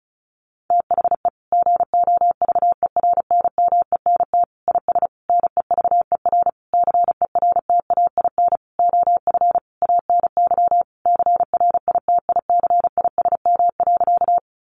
Morse code, heard as a keyed tone, about 34 wpm, 700 Hz. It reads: THE GOVERNMENT IS NEVER CERTAIN OF ANY CRITICISM.
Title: Sentences from Top 1300 Words 35wpm